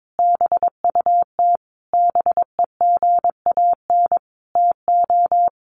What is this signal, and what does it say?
Morse code, heard as a keyed tone, about 22 wpm, 700 Hz. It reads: BUT BEGAN TO
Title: Sentences from Top 600 Words 22wpm